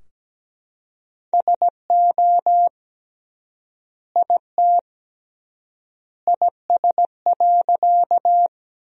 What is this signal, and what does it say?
Morse code, heard as a keyed tone, about 17 wpm, 700 Hz. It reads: SO IT IS.
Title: 3x Word-Spacing - Sentences from Top 100 Words - 17wpm